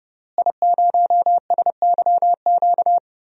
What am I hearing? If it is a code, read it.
Morse code, heard as a keyed tone, about 30 wpm, 700 Hz. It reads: I0SYQ
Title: DXCC-100 Call Signs - 5 Characters - Repeated 3x - 30wpm